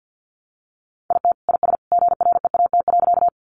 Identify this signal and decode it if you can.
Morse code, heard as a keyed tone, about 50 wpm, 700 Hz. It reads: IT IS GREAT.
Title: Sentences from Top 200 Words 50wpm